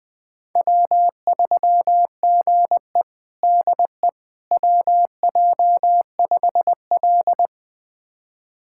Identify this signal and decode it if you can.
Morse code, heard as a keyed tone, about 20 wpm, 700 Hz. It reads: W3GE DE WJ5L
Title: USA Callsign DE Callsign - 4 Characters - 20wpm